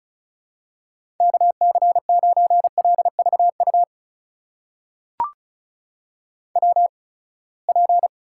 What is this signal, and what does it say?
Morse code, heard as a keyed tone, about 36 wpm, spaced as for 24 wpm, 700 Hz. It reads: KC9LVU WP
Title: USA Call Signs - 6 Characters - 8x Character Spacing - 35wpm